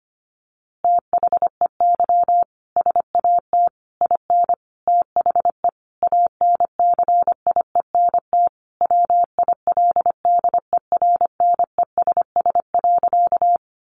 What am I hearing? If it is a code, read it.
Morse code, heard as a keyed tone, about 25 wpm, 700 Hz. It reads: THEY SAT IN THE ANCIENT WILDERNESS.
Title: A Sound Of Thunder 25wpm